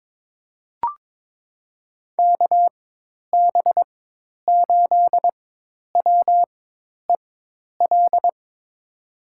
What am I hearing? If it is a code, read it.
Morse code, heard as a keyed tone, about 22 wpm, spaced as for 10 wpm, 700 Hz. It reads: KB8WEL
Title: USA Call Signs - 6 Characters - 4x Character Spacing - 22wpm